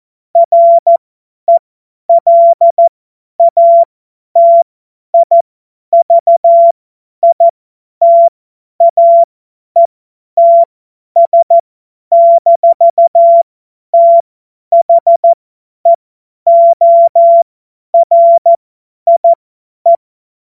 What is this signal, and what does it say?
Morse code, heard as a keyed tone, about 14 wpm, spaced as for 10 wpm, 680 Hz. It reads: RELATIVITAETS-THEORIE
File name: Morsekurs / Witze / Witz 12. April 2022 50 F70